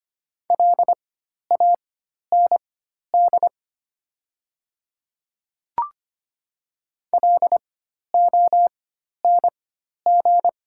Morse code, heard as a keyed tone, about 25 wpm, spaced as for 12 wpm, 700 Hz. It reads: LAND LONG
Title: 4x Character Spacing - Top 500 Words - 25wpm